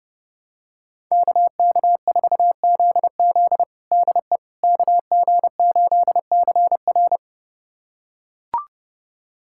Morse code, heard as a keyed tone, about 32 wpm, 700 Hz. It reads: KK4ZZ DE KG8CR
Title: USA Callsign DE Callsign - 5 Characters - 30wpm